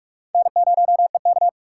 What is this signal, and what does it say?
Morse code, heard as a keyed tone, about 44 wpm, 700 Hz. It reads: N0EK
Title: USA Call Signs - 2x Character Spacing - 45wpm